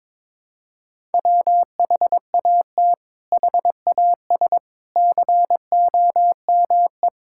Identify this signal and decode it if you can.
Morse code, heard as a keyed tone, about 22 wpm, 700 Hz. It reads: WHAT HAS COME
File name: practice / 22wpm / Sets of 8 Words - 22wpm